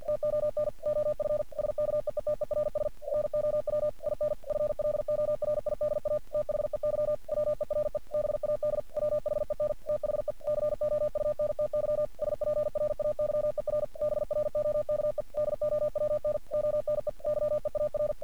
Morse code, heard as a keyed tone, about 46 wpm, 600 Hz. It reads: TON OF SKEETERS NOW IN FLORIDA THEY WERE BAD WHEN THE COUNTY SPRAYER BROKE DOWN ONE YEAR
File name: archived / CRSnet